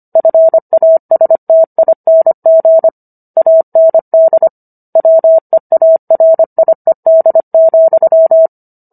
Morse code, heard as a keyed tone, about 25 wpm, 650 Hz. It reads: FASTING AND WEARIED,
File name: Book One / 16-Chapter 16 - 25 WPM